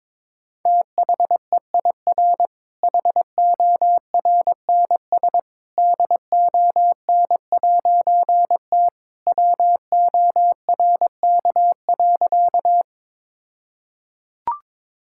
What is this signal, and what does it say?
Morse code, heard as a keyed tone, about 22 wpm, 700 Hz. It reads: THEIR HORNS DON'T WORK.
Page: morse-code-ninja-practice-22wpm